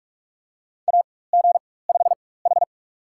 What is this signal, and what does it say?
Morse code, heard as a keyed tone, about 45 wpm, spaced as for 21 wpm, 700 Hz. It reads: AG5H
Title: USA Call Signs - 4 Characters - 4x Character Spacing - 45wpm